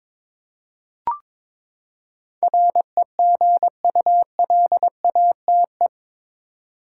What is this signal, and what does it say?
Morse code, heard as a keyed tone, about 22 wpm, 700 Hz. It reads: REGULATE
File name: practice / 22wpm / Eight Letter Words - 22wpm